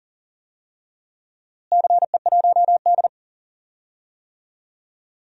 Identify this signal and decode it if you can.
Morse code, heard as a keyed tone, about 40 wpm, 700 Hz. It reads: CE1D
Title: DXCC-100 Call Signs - 4 Characters - Repeated 2x - 40wpm